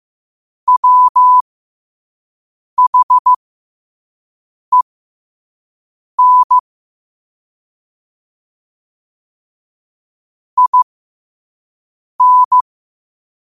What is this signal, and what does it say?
Morse code, heard as a keyed tone, about 15 wpm, spaced as for 5 wpm, 1000 Hz. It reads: WHEN IN
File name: qotdfiles / 5wpm